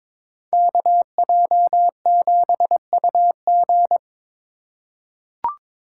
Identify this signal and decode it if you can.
Morse code, heard as a keyed tone, about 22 wpm, 700 Hz. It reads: KJ7UG
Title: USA Call Signs - 5 Characters - 3x Character Spacing - 22wpm